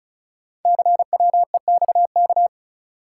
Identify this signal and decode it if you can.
Morse code, heard as a keyed tone, about 36 wpm, 690 Hz. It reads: CWEXK
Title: Code Groups - 5 Letters - 35wpm